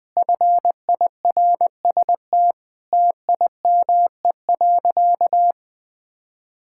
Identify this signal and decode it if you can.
Morse code, heard as a keyed tone, about 20 wpm, 700 Hz. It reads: FIRST TIME.
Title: Sentences with 4-Word Phrases - Spoken Beginning - 20wpm